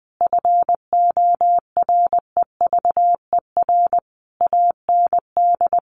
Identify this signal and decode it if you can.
Morse code, heard as a keyed tone, about 20 wpm, 700 Hz. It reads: FOREVER AND
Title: A Sound Of Thunder 20wpm